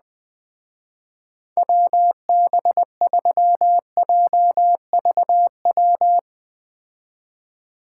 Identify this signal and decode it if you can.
Morse code, heard as a keyed tone, about 21 wpm, 700 Hz. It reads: WB3JVW